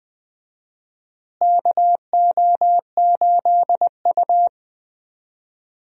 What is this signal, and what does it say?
Morse code, heard as a keyed tone, about 20 wpm, 700 Hz. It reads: KO8U
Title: USA Call Signs - 4 Characters - Repeated 3x - 20wpm